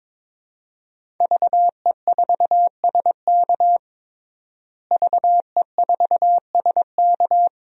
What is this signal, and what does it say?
Morse code, heard as a keyed tone, about 22 wpm, 700 Hz. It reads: VE4SK VE4SK
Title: DXCC-100 Call Signs - 5 Characters - Repeated 2x - 22wpm